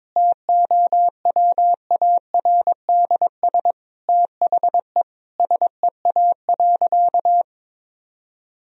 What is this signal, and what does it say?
Morse code, heard as a keyed tone, about 22 wpm, 700 Hz. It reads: TOWARDS THE SEA.
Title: Sentences from Top 1000 Words - Review Part 2 - 22wpm